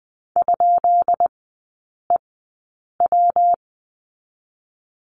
Morse code, heard as a keyed tone, about 19 wpm, spaced as for 8 wpm, 700 Hz. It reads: ?EW
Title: 184 Lesson 30e - Letter b - Three Characters 20wpm